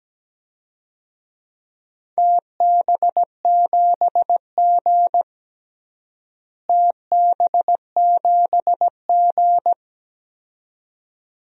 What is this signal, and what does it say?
Morse code, heard as a keyed tone, about 17 wpm, 700 Hz. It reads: TB7G TB7G
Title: DXCC-100 Call Signs - 4 Characters - Repeated 2x - 17wpm